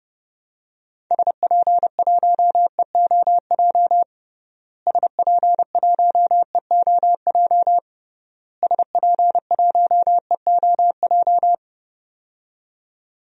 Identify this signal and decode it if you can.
Morse code, heard as a keyed tone, about 30 wpm, 700 Hz. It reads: SP1EOJ SP1EOJ SP1EOJ
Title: DXCC-100 Call Signs - 6 Characters - Repeated 3x - 30wpm